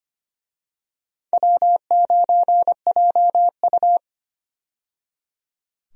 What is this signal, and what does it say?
Morse code, heard as a keyed tone, about 25 wpm, 700 Hz. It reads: W9JU
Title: USA Call Signs - 4 Characters - 25wpm